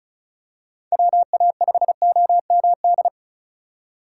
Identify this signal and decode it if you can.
Morse code, heard as a keyed tone, about 34 wpm, 700 Hz. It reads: WA5OMD
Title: USA Call Signs - 6 Characters - 2x Character Spacing - 35wpm